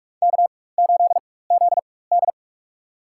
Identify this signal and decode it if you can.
Morse code, heard as a keyed tone, about 45 wpm, spaced as for 21 wpm, 700 Hz. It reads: K8ZD
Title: USA Call Signs - 4x Character Spacing - 45wpm